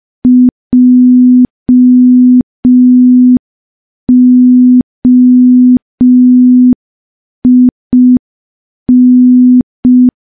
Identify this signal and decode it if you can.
Morse code, heard as a keyed tone, about 5 wpm, 250 Hz. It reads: JOIN
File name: morse / morse-training-05WPM